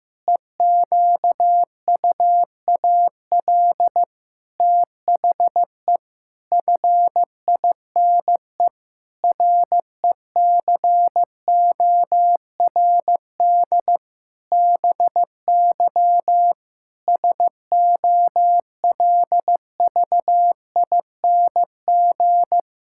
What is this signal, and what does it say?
Morse code, heard as a keyed tone, about 16 wpm, 700 Hz. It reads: EQUAL THE FINE RECORD BY SOLVING
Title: Pangram Challenge v2 - 15wpm